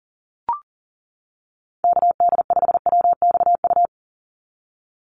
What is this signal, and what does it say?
Morse code, heard as a keyed tone, about 40 wpm, 700 Hz. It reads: KD5WXU
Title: USA Call Signs 40wpm